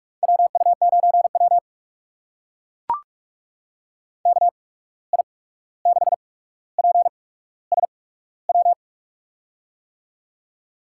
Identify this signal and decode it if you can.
Morse code, heard as a keyed tone, about 44 wpm, spaced as for 17 wpm, 700 Hz. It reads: WU9W KI6PSW
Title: USA Call Signs - 8x Character Spacing - 45wpm